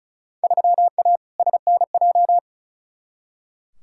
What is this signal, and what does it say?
Morse code, heard as a keyed tone, about 36 wpm, 700 Hz. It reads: 3A SNJ
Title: Field Day Exchanges - 35wpm